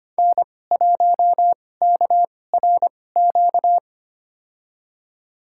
Morse code, heard as a keyed tone, about 25 wpm, spaced as for 18 wpm, 700 Hz. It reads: N1KRQ
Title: USA Call Signs - 5 Characters - 2x Character Spacing - 25wpm